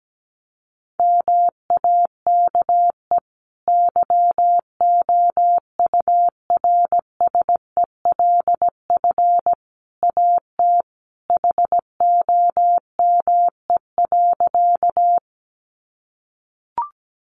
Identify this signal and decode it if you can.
Morse code, heard as a keyed tone, about 17 wpm, 700 Hz. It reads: MAKE YOURSELF AT HOME.
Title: Sentences from Top 600 Words 17wpm